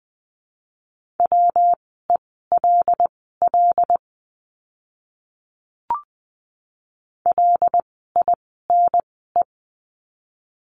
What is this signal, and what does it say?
Morse code, heard as a keyed tone, about 20 wpm, spaced as for 14 wpm, 700 Hz. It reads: WELL LINE
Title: 115 Lesson 19f - Letter w - Words 20wpm